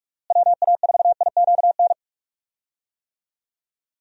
Morse code, heard as a keyed tone, about 46 wpm, 700 Hz. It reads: WA4IQN
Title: USA Call Signs - 6 Characters - Repeated 3x - 45wpm